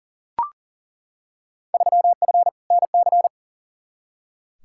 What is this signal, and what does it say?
Morse code, heard as a keyed tone, about 40 wpm, 700 Hz. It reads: 3F NC
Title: Field Day Exchanges - 40wpm